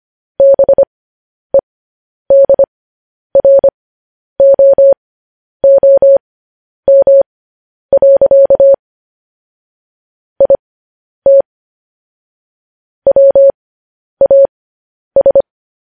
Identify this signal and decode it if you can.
Morse code, heard as a keyed tone, about 25 wpm, spaced as for 10 wpm, 550 Hz. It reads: BEDROOM. IT WAS